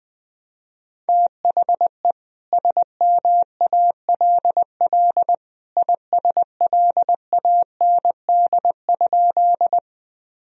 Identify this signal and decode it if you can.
Morse code, heard as a keyed tone, about 20 wpm, 700 Hz. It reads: THE SMALL ISLAND?
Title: Sentences from Top 1000 Words - Review Part 3 - 20wpm